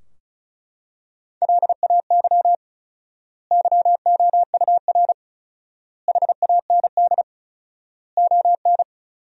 Morse code, heard as a keyed tone, about 34 wpm, 700 Hz. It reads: LAY YOUR HAND ON
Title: 4x Word-Spacing - Sentences from Top 400 Words - 35wpm